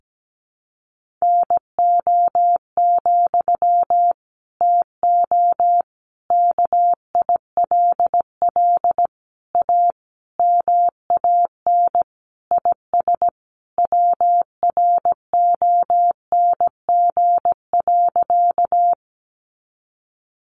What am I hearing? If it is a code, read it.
Morse code, heard as a keyed tone, about 17 wpm, 700 Hz. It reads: NO, TO KILL A MAN IS WRONG.
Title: Sentences from Top 1400 Words 17wpm